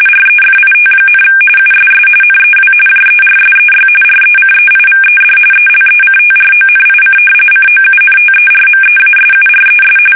RTTY, about 45.45 baud. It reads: REFIT, USS NEW JERSEY RECEIVED ELECTRONIC COUNTERMEASURES
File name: fsk850 / battleshipnj